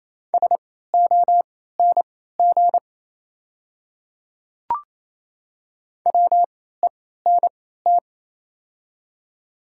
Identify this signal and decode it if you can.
Morse code, heard as a keyed tone, about 28 wpm, spaced as for 16 wpm, 700 Hz. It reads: SONG WENT
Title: 3x Character Spacing - Top 500 Words - 28wpm